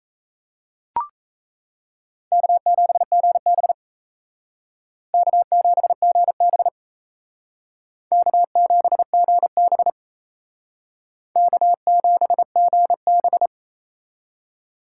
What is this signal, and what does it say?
Morse code, heard as a keyed tone, about 35 wpm, 700 Hz. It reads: K7GB K7GB K7GB K7GB
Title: Speed Racing - USA 1x2 Call Signs 28wpm